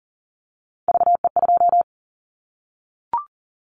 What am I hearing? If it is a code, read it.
Morse code, heard as a keyed tone, about 42 wpm, 700 Hz. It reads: VE2
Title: Callsign Prefixes for Top 50 DXCC - 40wpm